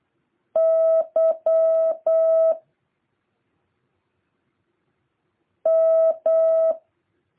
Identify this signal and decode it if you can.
Morse code, heard as a keyed tone, about 8 wpm, 650 Hz. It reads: Y M
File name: jh8cbh / 2023cw